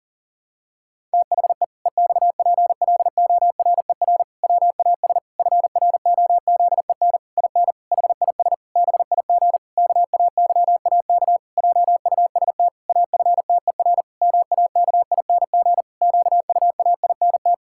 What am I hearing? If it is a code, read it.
Morse code, heard as a keyed tone, about 40 wpm, 700 Hz. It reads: THE EXPLORER WAS FROZEN IN HIS BIG KAYAK JUST AFTER MAKING QUAINT
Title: Pangram Challenge v2 - 40wpm